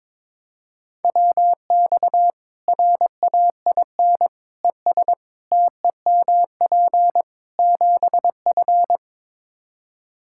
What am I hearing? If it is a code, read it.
Morse code, heard as a keyed tone, about 22 wpm, 700 Hz. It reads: WX RAIN ES TEMP 7F
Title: Practice QSO - Exchange 2 - Info - 22wpm